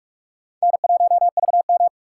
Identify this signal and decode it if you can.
Morse code, heard as a keyed tone, about 41 wpm, 700 Hz. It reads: N1VM
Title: USA Call Signs - 4 Characters - 6x Character Spacing - 45wpm